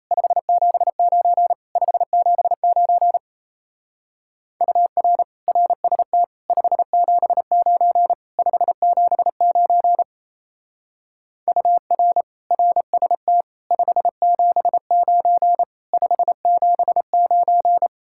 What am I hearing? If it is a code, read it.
Morse code, heard as a keyed tone, about 34 wpm, 700 Hz. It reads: 579 579 UR RST 579 579 UR RST 579 579
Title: Speed-Racing Common QSO Phrases Encore - 28wpm